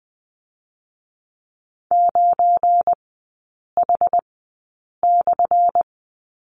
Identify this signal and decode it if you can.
Morse code, heard as a keyed tone, about 20 wpm, 700 Hz. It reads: 9 H /
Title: 218 Lesson 35e - Number 0 - Three Characters 20wpm